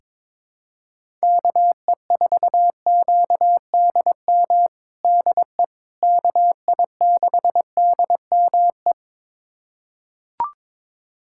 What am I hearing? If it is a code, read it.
Morse code, heard as a keyed tone, about 22 wpm, 700 Hz. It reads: KE4QDM DE KI6DME
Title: USA Callsign DE Callsign - 6 Characters - 22wpm